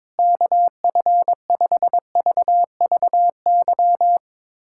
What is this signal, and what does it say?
Morse code, heard as a keyed tone, about 22 wpm, 700 Hz. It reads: KF5VVY
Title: USA Callsign DE Callsign - 6 Characters - 22wpm